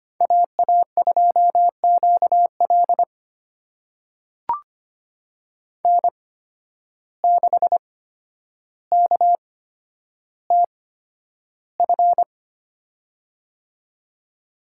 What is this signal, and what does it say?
Morse code, heard as a keyed tone, about 25 wpm, spaced as for 11 wpm, 700 Hz. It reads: AA2QL N6KTF
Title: USA Call Signs - 5 Characters - 8x Character Spacing - 25wpm